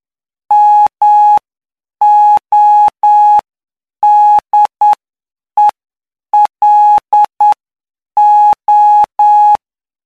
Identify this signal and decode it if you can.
Morse code, heard as a keyed tone, about 9 wpm, spaced as for 7 wpm, 840 Hz. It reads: MODELO
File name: cw / py2lcd / 08-PY2LCD-Li��o 17